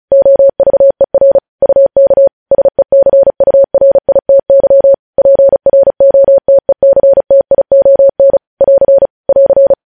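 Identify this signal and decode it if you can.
Morse code, heard as a keyed tone, about 36 wpm, 550 Hz. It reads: OVER UK SECURITY PROTECTION <AR> <AR>
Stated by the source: Morse news headlines practice at 35 words per minute.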